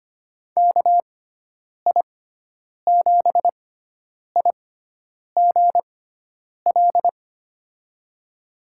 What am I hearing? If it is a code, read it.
Morse code, heard as a keyed tone, about 25 wpm, spaced as for 9 wpm, 700 Hz. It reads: KI7IGL
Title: USA Call Signs - 6x Character Spacing - 25wpm